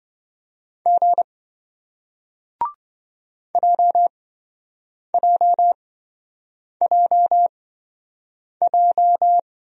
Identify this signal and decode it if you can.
Morse code, heard as a keyed tone, about 27 wpm, spaced as for 7 wpm, 700 Hz. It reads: G JJJJ